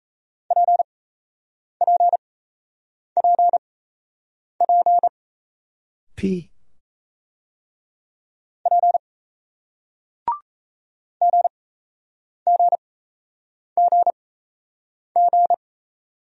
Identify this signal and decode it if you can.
Morse code, heard as a keyed tone, about 38 wpm, spaced as for 8 wpm, 700 Hz. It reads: PPPP P GGGG